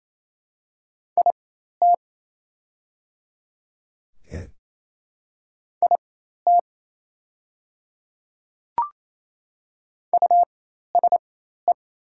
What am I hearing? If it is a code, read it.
Morse code, heard as a keyed tone, about 28 wpm, spaced as for 13 wpm, 700 Hz. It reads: IT IT USE
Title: 4x Character Spacing - Top 100 Words - 28wpm